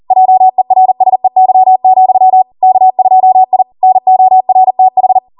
Morse code, heard as a keyed tone, about 40 wpm, 750 Hz. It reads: JERSEY, K2I NORTH